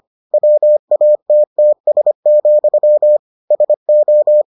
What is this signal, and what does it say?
Morse code, heard as a keyed tone, about 25 wpm, 600 Hz. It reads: WATTS, SO